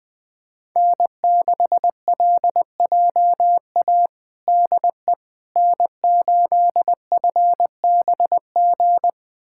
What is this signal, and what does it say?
Morse code, heard as a keyed tone, about 20 wpm, 700 Hz. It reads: N6LJA DE N8FBG
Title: USA Callsign DE Callsign - 5 Characters - 20wpm